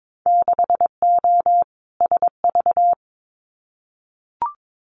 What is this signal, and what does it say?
Morse code, heard as a keyed tone, about 22 wpm, 700 Hz. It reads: 6O SV
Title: Winter Field Day Exchanges Encore 22wpm